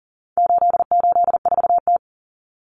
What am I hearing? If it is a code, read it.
Morse code, heard as a keyed tone, about 39 wpm, 700 Hz. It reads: 884T
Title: Common Cut Numbers - 4-Digit 40wpm